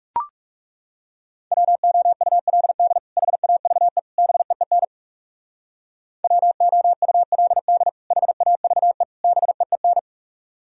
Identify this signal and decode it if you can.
Morse code, heard as a keyed tone, about 38 wpm, 700 Hz. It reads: WOULD HAVE BEEN WOULD HAVE BEEN